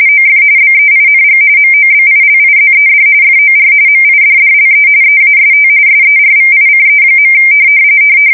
RTTY, about 45.45 baud. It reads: OF BEING MOBILIZED FOR MIDDLE EAST SERVICE.
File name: fsk170 / USS-Missouri